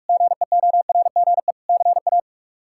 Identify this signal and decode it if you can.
Morse code, heard as a keyed tone, about 41 wpm, 700 Hz. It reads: GEORGE CA
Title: K1USN SST QSO Practice - 45wpm